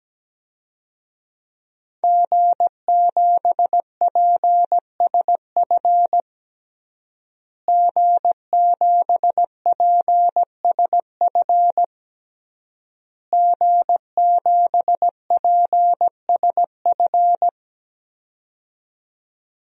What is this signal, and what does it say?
Morse code, heard as a keyed tone, about 17 wpm, 700 Hz. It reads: G7PSF G7PSF G7PSF
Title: DXCC-100 Call Signs - 5 Characters - Repeated 3x - 17wpm